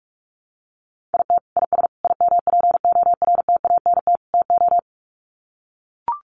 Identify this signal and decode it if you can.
Morse code, heard as a keyed tone, about 44 wpm, 700 Hz. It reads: IT IS IMPORTANT TO
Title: Sets of 4 Words 45wpm